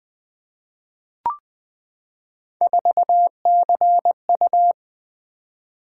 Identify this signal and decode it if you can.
Morse code, heard as a keyed tone, about 20 wpm, 700 Hz. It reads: 4CU
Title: Three Letter-Number Combinations - 20wpm